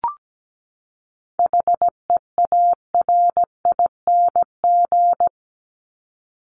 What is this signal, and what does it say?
Morse code, heard as a keyed tone, about 17 wpm, 700 Hz. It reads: HEARING
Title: ING Suffix - Top 100 Words with Base Words 17wpm